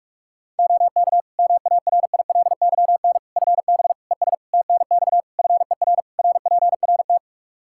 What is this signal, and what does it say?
Morse code, heard as a keyed tone, about 46 wpm, 700 Hz. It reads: OK MARILYN FB ES TNX FER RPRT
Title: Practice QSO - Exchange 2 - Info - 45wpm